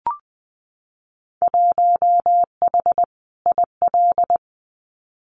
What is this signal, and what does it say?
Morse code, heard as a keyed tone, about 20 wpm, 700 Hz. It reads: 1H IL
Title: Winter Field Day Exchanges 20wpm